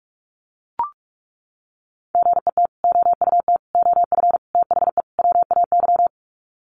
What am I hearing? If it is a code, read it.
Morse code, heard as a keyed tone, about 44 wpm, 700 Hz. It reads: GET OUT OF THE WAY
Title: Sets of 5 Words Encore 45wpm